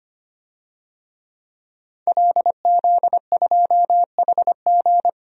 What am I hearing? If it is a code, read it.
Morse code, heard as a keyed tone, about 25 wpm, 700 Hz. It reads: LZ2HG
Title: DXCC-100 Call Signs - 5 Characters - 25wpm